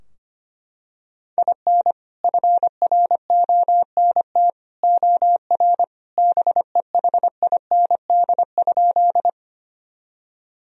Morse code, heard as a keyed tone, about 25 wpm, 700 Hz. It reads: IN FRONT OR BEHIND?
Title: Sentences from Top 1000 Words - Review Part 2 - 25wpm